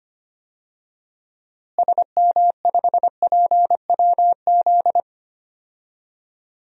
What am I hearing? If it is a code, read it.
Morse code, heard as a keyed tone, about 25 wpm, 700 Hz. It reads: SM5PWZ
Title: DXCC-100 Call Signs - 6 Characters - Repeated 3x - 25wpm